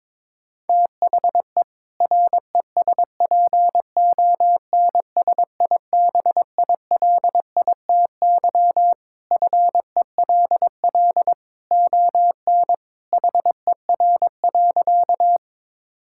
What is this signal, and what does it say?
Morse code, heard as a keyed tone, about 22 wpm, 700 Hz. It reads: THE RESPONSIBILITY FELL ON HER.
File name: practice / 22wpm / Sentences with 2-Word Phrases - 22wpm